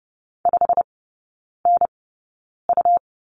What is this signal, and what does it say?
Morse code, heard as a keyed tone, about 30 wpm, spaced as for 9 wpm, 700 Hz. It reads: 5NU
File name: Lesson 18e - Letter m - Three Characters 30wpm